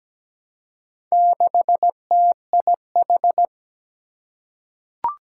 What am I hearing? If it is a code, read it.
Morse code, heard as a keyed tone, about 17 wpm, 700 Hz. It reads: 6TIH
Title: Four Challenging Letter-Number Combinations 17wpm